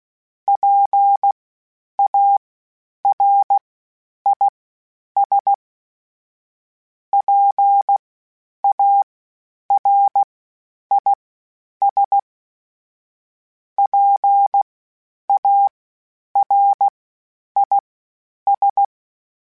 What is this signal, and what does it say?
Morse code, heard as a keyed tone, about 16 wpm, spaced as for 9 wpm, 800 Hz. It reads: PARIS PARIS PARIS
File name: PARIS Calibration / PARIS mp3s / PARIS@10-@15